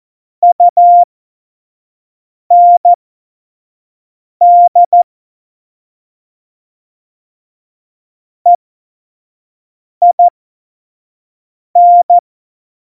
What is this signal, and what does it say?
Morse code, heard as a keyed tone, about 14 wpm, spaced as for 5 wpm, 700 Hz. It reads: UND EIN
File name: Morsekurs / Witze / Witz 11. Februar 2020, 25 F70